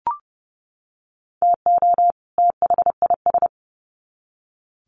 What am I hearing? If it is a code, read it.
Morse code, heard as a keyed tone, about 28 wpm, 700 Hz. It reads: TO THIS
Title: Sets of 2 Words 30wpm